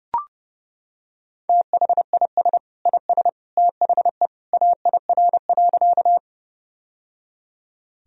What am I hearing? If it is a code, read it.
Morse code, heard as a keyed tone, about 32 wpm, 700 Hz. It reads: THIS IS THE AIR.
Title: Sentences from Top 1000 Words - Review Part 1 - 30wpm